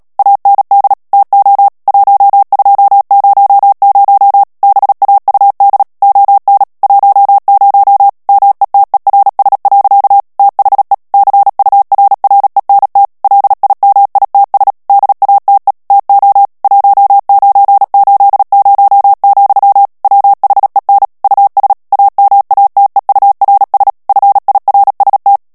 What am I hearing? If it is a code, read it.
Morse code, heard as a keyed tone, about 37 wpm, 800 Hz. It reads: AND TO 1200 BAUD ON 10 METERS. THE CURRENT LIMITS DATE TO 1980, WHEN US AMATEURS FIRST
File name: morse mp3 / ANGLAIS texte clair 3 vitesse 35 mots minute